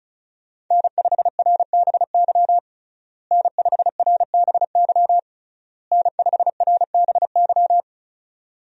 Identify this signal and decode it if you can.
Morse code, heard as a keyed tone, about 34 wpm, 700 Hz. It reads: N5RBY N5RBY N5RBY
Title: USA Call Signs - 5 Characters - Repeated 3x - 35wpm